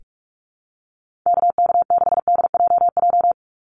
Morse code, heard as a keyed tone, about 44 wpm, 700 Hz. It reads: KK6DJJ
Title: USA Call Signs 45wpm